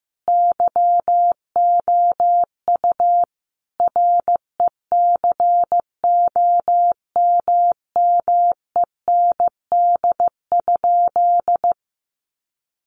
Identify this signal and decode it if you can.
Morse code, heard as a keyed tone, about 14 wpm, 700 Hz. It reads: YOU RECOMMEND?
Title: Sentences from Top 1200 Words 15wpm